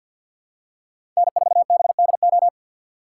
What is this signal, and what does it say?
Morse code, heard as a keyed tone, about 53 wpm, 700 Hz. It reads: N4BDO
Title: USA Call Signs - 5 Characters - 2x Character Spacing - 50wpm